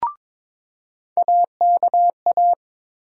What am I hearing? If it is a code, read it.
Morse code, heard as a keyed tone, about 22 wpm, 700 Hz. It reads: AKA
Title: USA Call Sign Postfixes - 3 Letters 22wpm